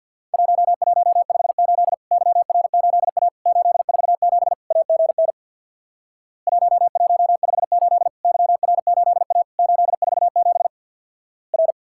Morse code, heard as a keyed tone, about 50 wpm, 700 Hz. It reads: 1158 YR8A 847 AGN 1158 YR8A 847 R
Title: Worked All Europe DX Contest - QTC Exchange with Repeats - 50wpm